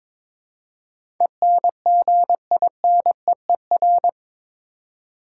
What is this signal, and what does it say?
Morse code, heard as a keyed tone, about 22 wpm, 700 Hz. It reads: ENGINEER
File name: practice / 22wpm / Eight Letter Words - 22wpm